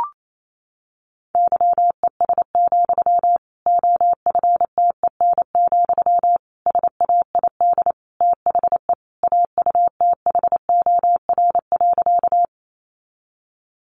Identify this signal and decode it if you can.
Morse code, heard as a keyed tone, about 28 wpm, 700 Hz. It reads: YES, OFTEN, SAID THE AUTHOR.
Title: Sentences from Top 1200 Words 28wpm